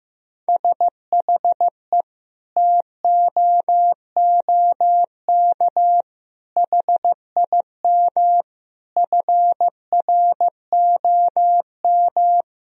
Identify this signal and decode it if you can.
Morse code, heard as a keyed tone, about 15 wpm, 700 Hz. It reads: SHE TOOK HIM FROM
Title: Sentences from Top 1000 Words - Review Part 4 - 15wpm